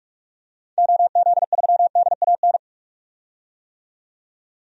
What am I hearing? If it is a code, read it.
Morse code, heard as a keyed tone, about 45 wpm, 700 Hz. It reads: OZ3DAN
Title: DXCC-100 Call Signs - 6 Characters - 45wpm